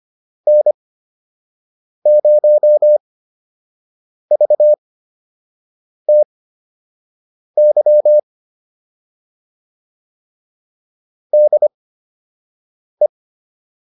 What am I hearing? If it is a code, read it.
Morse code, heard as a keyed tone, about 25 wpm, spaced as for 6 wpm, 600 Hz. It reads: N0VTY DE